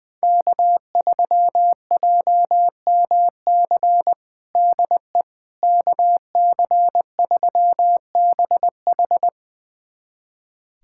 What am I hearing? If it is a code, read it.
Morse code, heard as a keyed tone, about 20 wpm, 700 Hz. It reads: K3JMC DE KC3BH
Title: USA Callsign DE Callsign - 5 Characters - 20wpm